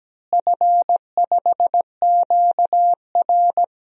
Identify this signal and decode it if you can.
Morse code, heard as a keyed tone, about 17 wpm, 700 Hz. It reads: F5QR
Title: DXCC-100 Call Signs - 4 Characters - Repeated 2x - 17wpm